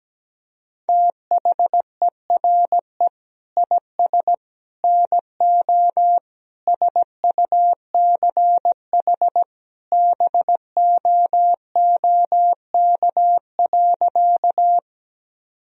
Morse code, heard as a keyed tone, about 17 wpm, 700 Hz. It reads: THERE IS NO SUCH BOOK.
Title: Sentences from Top 1000 Words - Review Part 1 - 17wpm